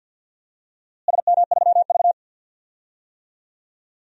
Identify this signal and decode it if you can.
Morse code, heard as a keyed tone, about 48 wpm, 700 Hz. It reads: SM3V
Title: DXCC-100 Call Signs - 4 Characters - Repeated 3x - 50wpm